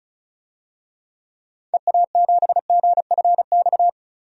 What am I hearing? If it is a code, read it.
Morse code, heard as a keyed tone, about 34 wpm, 700 Hz. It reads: EA7GFX
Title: DXCC-100 Call Signs - 6 Characters - 35wpm